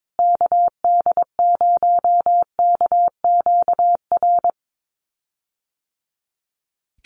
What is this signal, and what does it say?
Morse code, heard as a keyed tone, about 22 wpm, 700 Hz. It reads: KD0KQR
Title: USA Call Signs Repeated 2x 22wpm